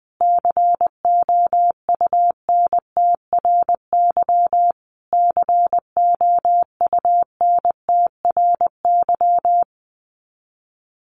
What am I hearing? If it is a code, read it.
Morse code, heard as a keyed tone, about 20 wpm, 700 Hz. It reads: COUNTRY COUNTRY
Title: Top 300 Words Repeated 20wpm